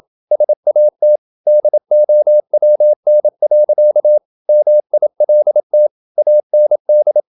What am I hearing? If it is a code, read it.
Morse code, heard as a keyed tone, about 27 wpm, 600 Hz. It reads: SAT DOWN. MILT AND
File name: Never 0027wpm